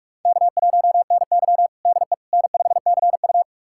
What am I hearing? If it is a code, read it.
Morse code, heard as a keyed tone, about 46 wpm, 700 Hz. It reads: K1NY DE N5CU
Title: USA Callsign DE Callsign - 4 Characters - 45wpm